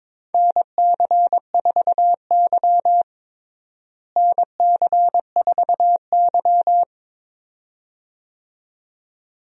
Morse code, heard as a keyed tone, about 22 wpm, 700 Hz. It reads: NC4Y NC4Y
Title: USA Call Signs - 4 Characters - Repeated 2x - 22wpm